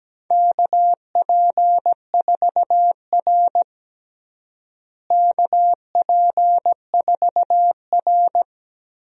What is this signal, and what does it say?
Morse code, heard as a keyed tone, about 17 wpm, 700 Hz. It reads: KP4R KP4R
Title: DXCC-100 Call Signs - 4 Characters - Repeated 3x - 17wpm